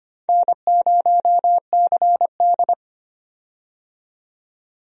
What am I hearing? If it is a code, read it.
Morse code, heard as a keyed tone, about 25 wpm, 700 Hz. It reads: N0CD
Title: USA Call Signs - 4 Characters - 25wpm